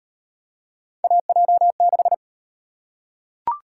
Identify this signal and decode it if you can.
Morse code, heard as a keyed tone, about 37 wpm, 700 Hz. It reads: AJ6